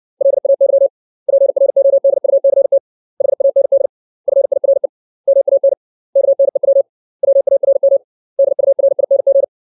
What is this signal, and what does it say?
Morse code, heard as a keyed tone, about 60 wpm, 550 Hz. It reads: LAY PRODUCT HAND FIRE MAN KNEW WARM DURING
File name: 632 words / 632 Words 60WPM